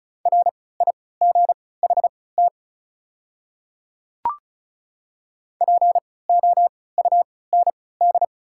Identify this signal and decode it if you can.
Morse code, heard as a keyed tone, about 34 wpm, spaced as for 20 wpm, 690 Hz. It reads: RIGHT POUND
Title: 3x Character Spacing - Top 500 Words - 35wpm